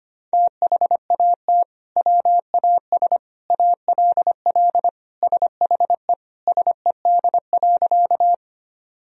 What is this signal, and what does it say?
Morse code, heard as a keyed tone, about 25 wpm, 700 Hz. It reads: THAT WAS ALL SHE SED.
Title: Sentences from Top 100 Words and Abbreviations 25wpm